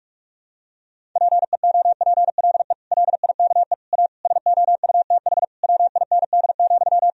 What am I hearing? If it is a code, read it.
Morse code, heard as a keyed tone, about 46 wpm, 700 Hz. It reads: PEOPLE LIKE A SOUTH WIND,
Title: Sentences from Top 1000 Words - Review Part 3 - 45wpm